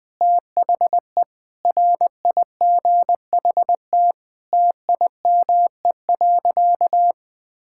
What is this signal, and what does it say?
Morse code, heard as a keyed tone, about 20 wpm, 700 Hz. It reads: THE RIGHT TIME.
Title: Sentences with 9-Word Phrases - 20wpm